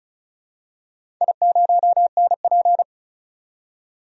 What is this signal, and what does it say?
Morse code, heard as a keyed tone, about 36 wpm, 700 Hz. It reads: I0NP
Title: DXCC-100 Call Signs - 4 Characters - 35wpm